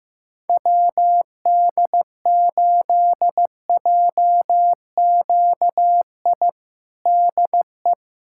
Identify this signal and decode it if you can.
Morse code, heard as a keyed tone, about 16 wpm, 700 Hz. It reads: WD8JQI DE
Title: USA Callsign DE Callsign 15wpm